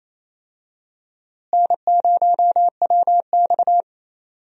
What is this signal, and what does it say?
Morse code, heard as a keyed tone, about 28 wpm, 700 Hz. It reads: N0WX
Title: USA Call Signs - 4 Characters - Repeated 3x - 28wpm